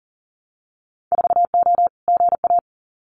Morse code, heard as a keyed tone, about 40 wpm, 700 Hz. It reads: 4O GA
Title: Winter Field Day Exchanges 40wpm